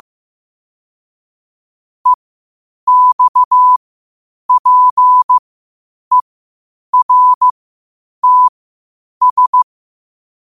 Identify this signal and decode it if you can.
Morse code, heard as a keyed tone, about 15 wpm, spaced as for 8 wpm, 1000 Hz. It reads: EXPERTS
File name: qotdfiles / 8wpm